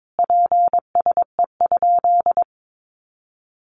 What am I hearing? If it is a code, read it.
Morse code, heard as a keyed tone, about 22 wpm, 700 Hz. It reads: PSE?
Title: Common QSO Phrases 22wpm